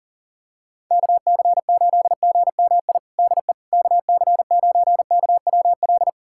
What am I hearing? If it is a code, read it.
Morse code, heard as a keyed tone, about 40 wpm, 700 Hz. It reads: KC8GMI DE KC9KWL
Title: USA Callsign DE Callsign - 6 Characters - 40wpm